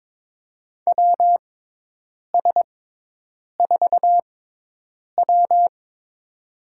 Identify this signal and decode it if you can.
Morse code, heard as a keyed tone, about 21 wpm, spaced as for 8 wpm, 700 Hz. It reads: WS4W
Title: USA Call Signs - 6x Character Spacing - 22wpm